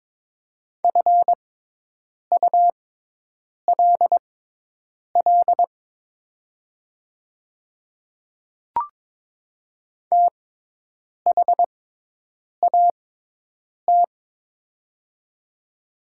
Morse code, heard as a keyed tone, about 22 wpm, spaced as for 8 wpm, 700 Hz. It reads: FULL THAT
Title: 6x Character Spacing - Top 500 Words - 22wpm